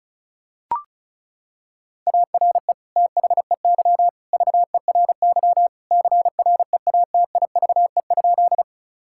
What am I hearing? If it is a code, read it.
Morse code, heard as a keyed tone, about 34 wpm, 700 Hz. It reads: ARE THEY VERY CREATIVE?
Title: Sentences from Top 1000 Words - Review Part 5 - 35wpm